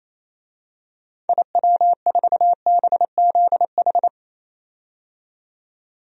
Morse code, heard as a keyed tone, about 28 wpm, 700 Hz. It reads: IW4BZH
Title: DXCC-100 Call Signs - 6 Characters - 28wpm